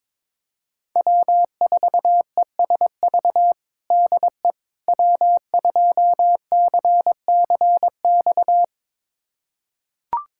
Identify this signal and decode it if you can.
Morse code, heard as a keyed tone, about 22 wpm, 700 Hz. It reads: W4ESV DE W2CCX
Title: USA Callsign DE Callsign - 5 Characters - 22wpm